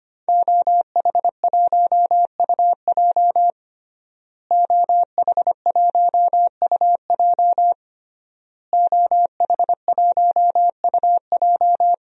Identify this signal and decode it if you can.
Morse code, heard as a keyed tone, about 25 wpm, 700 Hz. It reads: OH1UJ OH1UJ OH1UJ
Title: DXCC-100 Call Signs - 5 Characters - Repeated 3x - 25wpm